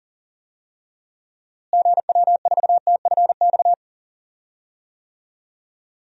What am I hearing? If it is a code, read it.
Morse code, heard as a keyed tone, about 40 wpm, 700 Hz. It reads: GW4TFX
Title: DXCC-100 Call Signs - 6 Characters - Repeated 2x - 40wpm